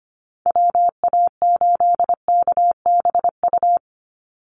Morse code, heard as a keyed tone, about 25 wpm, 700 Hz. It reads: WA8KBU
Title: Call Signs Repeated 3x - 25wpm